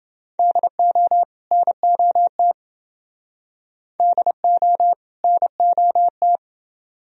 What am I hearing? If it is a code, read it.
Morse code, heard as a keyed tone, about 30 wpm, 700 Hz. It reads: DO NOT DO NOT